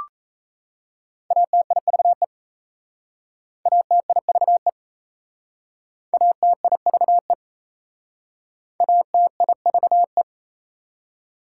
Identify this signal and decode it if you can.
Morse code, heard as a keyed tone, about 35 wpm, 690 Hz. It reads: ATIVE ATIVE ATIVE ATIVE
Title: Speed-Racing Prefixes and Suffixes 28wpm